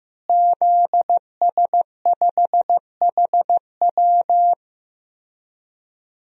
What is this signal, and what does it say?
Morse code, heard as a keyed tone, about 15 wpm, 700 Hz. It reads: ZS5HW
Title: DXCC-100 Call Signs - 5 Characters - Repeated 3x - 15wpm